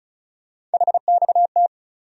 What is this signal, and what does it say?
Morse code, heard as a keyed tone, about 36 wpm, 700 Hz. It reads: HXT
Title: USA Call Sign Postfixes - 3 Letters 35wpm